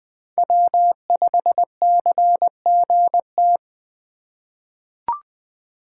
Morse code, heard as a keyed tone, about 20 wpm, 700 Hz. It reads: W5CGT
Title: USA Call Signs - 5 Characters - 4x Character Spacing - 20wpm